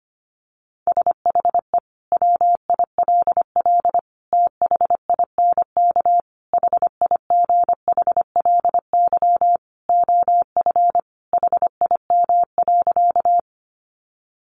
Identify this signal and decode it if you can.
Morse code, heard as a keyed tone, about 25 wpm, 700 Hz. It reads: SHE WILL THINK HIGHLY OF HIM.
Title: Sentences from Top 1100 Words 25wpm